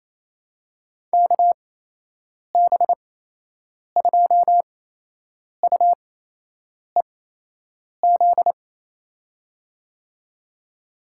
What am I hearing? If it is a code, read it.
Morse code, heard as a keyed tone, about 28 wpm, spaced as for 8 wpm, 700 Hz. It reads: KB2UEZ
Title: USA Call Signs - 8x Character Spacing - 28wpm